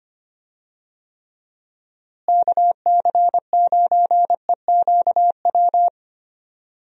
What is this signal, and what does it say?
Morse code, heard as a keyed tone, about 25 wpm, 700 Hz. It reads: KC9EQW
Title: USA Call Signs - 6 Characters - Repeated 2x - 25wpm